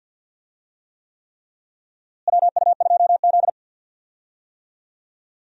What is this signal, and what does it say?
Morse code, heard as a keyed tone, about 53 wpm, 700 Hz. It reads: WU2Z
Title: USA Call Signs - 4 Characters - Repeated 3x - 50wpm